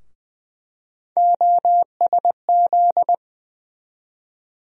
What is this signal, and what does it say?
Morse code, heard as a keyed tone, about 20 wpm, 700 Hz. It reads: OSZ
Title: USA Call Sign Postfixes - 3 Letters 20wpm